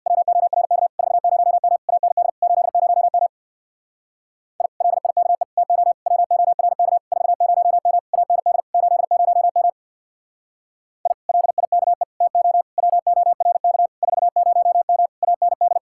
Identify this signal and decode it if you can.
Morse code, heard as a keyed tone, about 64 wpm, 700 Hz. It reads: WORK 40M AND 80M I LIKE TO WORK 40M AND 80M I LIKE TO WORK 40M AND